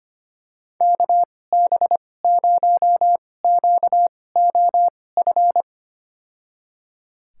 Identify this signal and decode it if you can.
Morse code, heard as a keyed tone, about 25 wpm, spaced as for 18 wpm, 700 Hz. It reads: KB0QOF
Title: USA Call Signs - 6 Characters - 2x Character Spacing - 25wpm